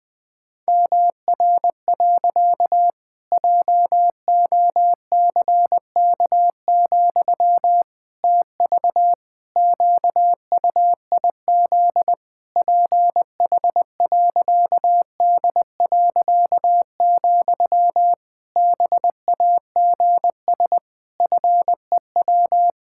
Morse code, heard as a keyed tone, about 21 wpm, 700 Hz. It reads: MR. JOCK, TV QUIZ PH.D., BAGS FEW
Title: Pangram Challenge v2 - 20wpm